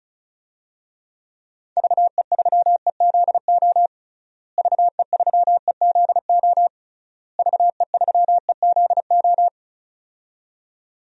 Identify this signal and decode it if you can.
Morse code, heard as a keyed tone, about 36 wpm, 700 Hz. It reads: VE3EZO VE3EZO VE3EZO
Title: DXCC-100 Call Signs - 6 Characters - Repeated 3x - 35wpm